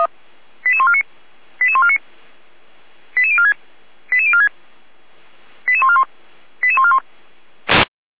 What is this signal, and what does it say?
Five-tone selective call (ZVEI), heard as quick runs of five tones, tones of about 70 ms. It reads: 88149 88149 88046 88046 88141 88141
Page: /455/SDR/ZVEI/88397/2025.06.23/